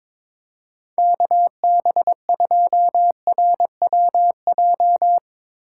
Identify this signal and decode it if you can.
Morse code, heard as a keyed tone, about 21 wpm, 700 Hz. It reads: KB2RWJ
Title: USA Call Signs - 6 Characters - 4x Character Spacing - 22wpm